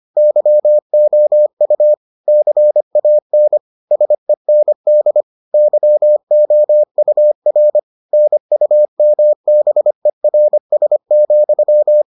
Morse code, heard as a keyed tone, about 25 wpm, 600 Hz. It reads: YOU CAN SEND YOUR NUMBERS,